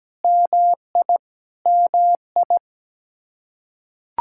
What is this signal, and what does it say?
Morse code, heard as a keyed tone, about 17 wpm, 700 Hz. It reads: MI MI
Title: Practice QSO - Exchange 1 - Beginning - 17wpm